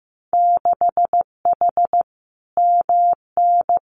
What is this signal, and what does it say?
Morse code, heard as a keyed tone, about 15 wpm, 700 Hz. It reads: 6H MN
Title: Winter Field Day Exchanges Encore 15wpm